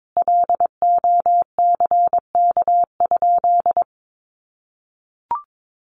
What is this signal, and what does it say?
Morse code, heard as a keyed tone, about 22 wpm, 700 Hz. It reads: LOCK?
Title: Sentences from Top 1300 Words 22wpm